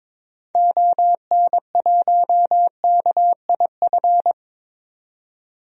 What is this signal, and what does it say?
Morse code, heard as a keyed tone, about 22 wpm, 700 Hz. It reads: ON1KIF
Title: DXCC-100 Call Signs - 6 Characters - 22wpm